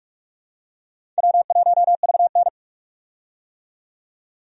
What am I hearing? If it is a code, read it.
Morse code, heard as a keyed tone, about 47 wpm, 700 Hz. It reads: W1VN
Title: USA Call Signs - 4 Characters - 45wpm